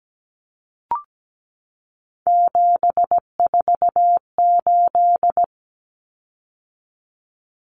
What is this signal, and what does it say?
Morse code, heard as a keyed tone, about 17 wpm, 690 Hz. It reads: 748
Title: Common Cut Numbers - 3-Digit 17wpm